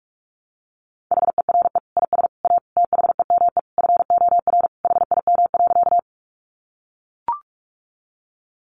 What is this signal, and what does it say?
Morse code, heard as a keyed tone, about 44 wpm, 700 Hz. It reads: HERE IS A THEME FOR HIM.
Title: Sentences from Top 1100 Words 45wpm